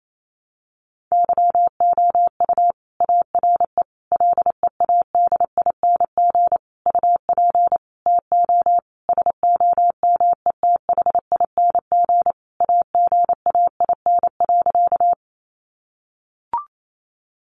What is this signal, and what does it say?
Morse code, heard as a keyed tone, about 28 wpm, 700 Hz. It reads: YOU ARE LEADING UP TO SOMETHING AGAIN.
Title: Sentences from Top 1100 Words 28wpm